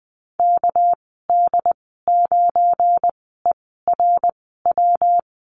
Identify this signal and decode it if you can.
Morse code, heard as a keyed tone, about 20 wpm, spaced as for 15 wpm, 700 Hz. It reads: KD9ERW
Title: 200 Lesson 32g - Letter k - Call Signs 20wpm